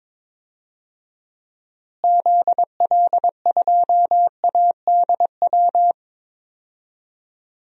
Morse code, heard as a keyed tone, about 22 wpm, 700 Hz. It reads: ZL2ADW
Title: DXCC-100 Call Signs - 6 Characters - Repeated 3x - 22wpm